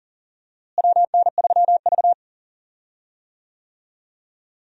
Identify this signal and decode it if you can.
Morse code, heard as a keyed tone, about 40 wpm, 700 Hz. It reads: WN3V
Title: USA Call Signs - 4 Characters - Repeated 3x - 40wpm